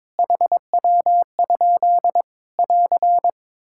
Morse code, heard as a keyed tone, about 21 wpm, 700 Hz. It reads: HW? <AR>
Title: K1USN SST QSO Practice 22wpm